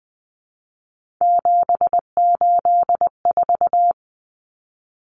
Morse code, heard as a keyed tone, about 20 wpm, 700 Hz. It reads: 784
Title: Numbers - 3-Digit 20wpm